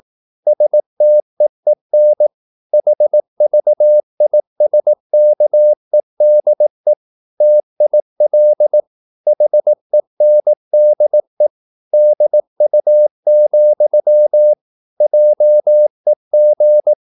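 Morse code, heard as a keyed tone, about 18 wpm, 600 Hz. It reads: STEEN HVISKEDE TIL HENDE DU, JEG
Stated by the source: Slow (QRS) Morse practice at 18 wpm, CW med Gnister.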